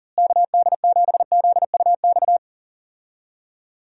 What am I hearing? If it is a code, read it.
Morse code, heard as a keyed tone, about 40 wpm, 700 Hz. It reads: KD7ZUX
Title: USA Call Signs - 6 Characters - Repeated 2x - 40wpm